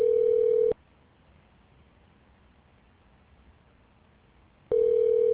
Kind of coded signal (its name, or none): telephone call progress tone